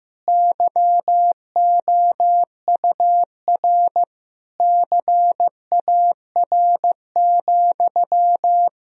Morse code, heard as a keyed tone, about 15 wpm, 700 Hz. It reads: YOUR CAR,
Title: Sentences from Top 1000 Words - Review Part 1 - 15wpm